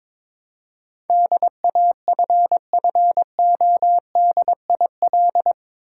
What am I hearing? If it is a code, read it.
Morse code, heard as a keyed tone, about 21 wpm, 700 Hz. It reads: DAFFODIL
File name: Difficult Words with Hints - 22wpm